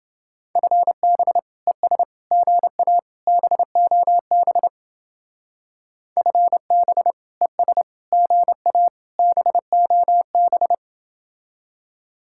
Morse code, heard as a keyed tone, about 29 wpm, 700 Hz. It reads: FB ES GA BOB FB ES GA BOB